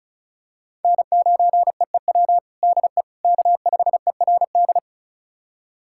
Morse code, heard as a keyed tone, about 36 wpm, 700 Hz. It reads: N9EEW DE K5ERD
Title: USA Callsign DE Callsign - 5 Characters - 35wpm